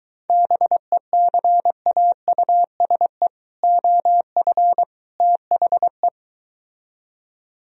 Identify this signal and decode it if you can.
Morse code, heard as a keyed tone, about 23 wpm, 700 Hz. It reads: BECAUSE OF THE